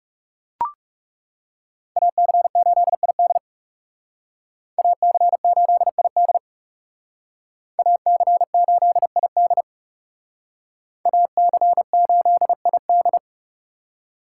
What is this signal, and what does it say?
Morse code, heard as a keyed tone, about 38 wpm, 700 Hz. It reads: AC8ID AC8ID AC8ID AC8ID